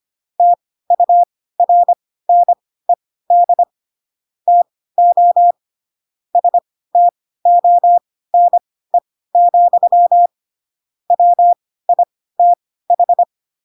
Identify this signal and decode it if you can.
Morse code, heard as a keyed tone, about 25 wpm, spaced as for 16 wpm, 700 Hz. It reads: TURNED TO STONE, WITH